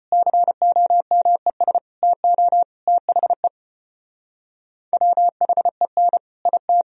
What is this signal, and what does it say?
Morse code, heard as a keyed tone, about 32 wpm, 700 Hz. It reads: COMES TO THE WHEN IT